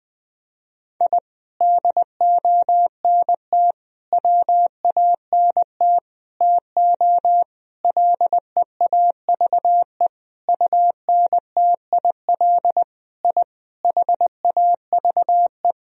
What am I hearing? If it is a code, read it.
Morse code, heard as a keyed tone, about 21 wpm, 700 Hz. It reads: I DONT WANT TO LEAVE UNTIL I HAVE
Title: Sentences from Top 1000 Words - Review Part 1 - 20wpm